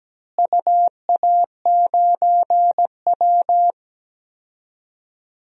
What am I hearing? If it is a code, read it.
Morse code, heard as a keyed tone, about 17 wpm, 700 Hz. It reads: UA9W
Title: DXCC-100 Call Signs - 4 Characters - Repeated 2x - 17wpm